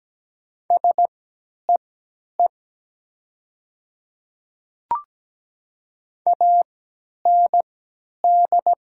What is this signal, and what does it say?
Morse code, heard as a keyed tone, about 17 wpm, spaced as for 10 wpm, 700 Hz. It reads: SEE AND
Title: 3x Character Spacing - Top 200 Words - 17wpm